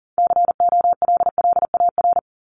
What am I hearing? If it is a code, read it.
Morse code, heard as a keyed tone, about 40 wpm, 700 Hz. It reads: COLLAR
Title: Top 1751-2000 Words 40wpm